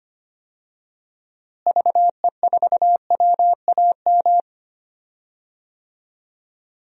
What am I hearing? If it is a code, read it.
Morse code, heard as a keyed tone, about 25 wpm, 700 Hz. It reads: VE4WAM
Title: DXCC-100 Call Signs - 6 Characters - Repeated 3x - 25wpm